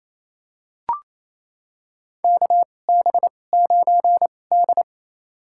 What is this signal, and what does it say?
Morse code, heard as a keyed tone, about 28 wpm, spaced as for 20 wpm, 700 Hz. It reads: KB9D
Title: USA Call Signs - 4 Characters - 2x Character Spacing - 28wpm